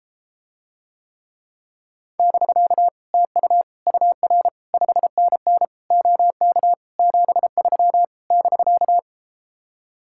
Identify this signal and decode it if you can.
Morse code, heard as a keyed tone, about 33 wpm, 700 Hz. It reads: <BK> TU UR 5NN OK 73 <BK>